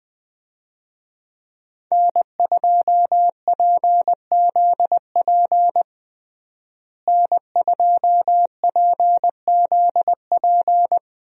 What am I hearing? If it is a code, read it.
Morse code, heard as a keyed tone, about 20 wpm, 700 Hz. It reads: N2PZP N2PZP
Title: USA Call Signs - 5 Characters - Repeated 2x - 20wpm